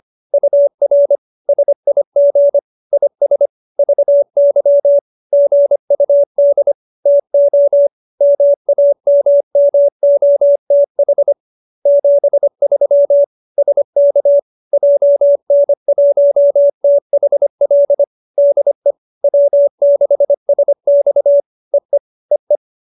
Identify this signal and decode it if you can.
Morse code, heard as a keyed tone, about 25 wpm, 570 Hz. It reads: UR SIG IS VY GUD TO MAMMOTH 73 SK JN1THL DE W6SX EE EE